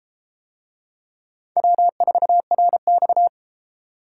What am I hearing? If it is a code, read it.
Morse code, heard as a keyed tone, about 31 wpm, 700 Hz. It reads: W4RX